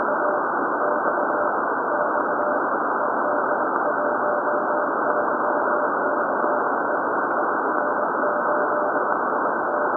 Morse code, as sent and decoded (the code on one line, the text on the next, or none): none